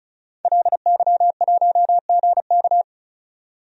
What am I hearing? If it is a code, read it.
Morse code, heard as a keyed tone, about 36 wpm, 700 Hz. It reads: LY1GK
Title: DXCC-100 Call Signs - 5 Characters - 35wpm